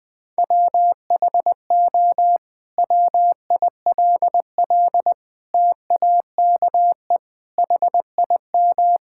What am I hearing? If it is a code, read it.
Morse code, heard as a keyed tone, about 21 wpm, 700 Hz. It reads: WHO WILL TAKE HIM
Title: Sentences from Top 1600 Words 20wpm